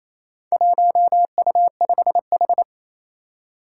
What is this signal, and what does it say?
Morse code, heard as a keyed tone, about 28 wpm, 700 Hz. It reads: 1U5H
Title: Four Challenging Letter-Number Combinations 28wpm